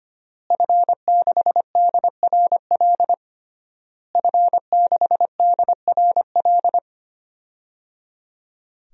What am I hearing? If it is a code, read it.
Morse code, heard as a keyed tone, about 25 wpm, 700 Hz. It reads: F6DRL F6DRL
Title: DXCC-100 Call Signs - 5 Characters - Repeated 3x - 25wpm